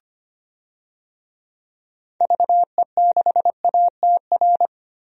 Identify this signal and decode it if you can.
Morse code, heard as a keyed tone, about 25 wpm, 700 Hz. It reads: VE6ATR
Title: DXCC-100 Call Signs - 6 Characters - 25wpm